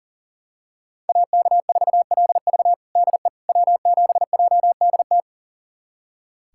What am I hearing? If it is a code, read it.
Morse code, heard as a keyed tone, about 40 wpm, 700 Hz. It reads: AK4LV DE W7JDT
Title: USA Callsign DE Callsign - 5 Characters - 40wpm